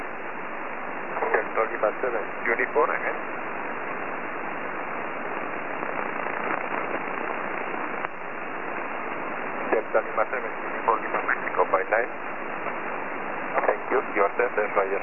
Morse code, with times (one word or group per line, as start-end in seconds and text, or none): none